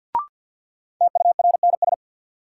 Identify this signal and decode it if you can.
Morse code, heard as a keyed tone, about 52 wpm, 700 Hz. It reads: TURNS
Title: Sets of 5 Words Encore - New Word Drill - 50wpm